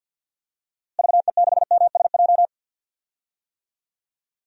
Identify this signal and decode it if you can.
Morse code, heard as a keyed tone, about 53 wpm, 700 Hz. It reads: VE6MSJ
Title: DXCC-100 Call Signs - 6 Characters - Repeated 3x - 50wpm